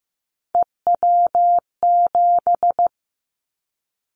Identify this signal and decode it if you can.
Morse code, heard as a keyed tone, about 15 wpm, 700 Hz. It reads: EW7
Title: Callsign Prefixes for Top 50 DXCC - 15wpm